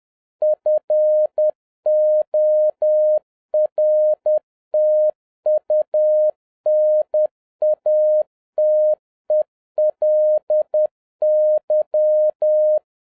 Morse code, heard as a keyed tone, about 10 wpm, 610 Hz. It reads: FORTUNATELY